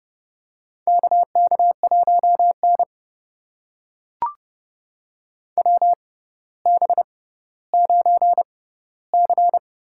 Morse code, heard as a keyed tone, about 30 wpm, spaced as for 15 wpm, 700 Hz. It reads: KK1N WB9C
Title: USA Call Signs - 6x Character Spacing - 30wpm